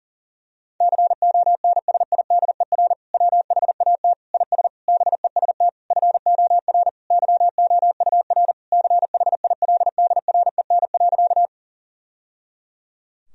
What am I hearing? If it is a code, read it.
Morse code, heard as a keyed tone, about 40 wpm, 700 Hz. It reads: CONSIDER WHAT IS BEST FOR YOUR CHILDREN.
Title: Sentences from Top 1000 Words - Review Part 4 - 40wpm